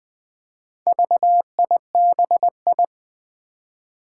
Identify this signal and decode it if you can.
Morse code, heard as a keyed tone, about 20 wpm, 700 Hz. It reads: VIBI
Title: Four Challenging Letter-Number Combinations 20wpm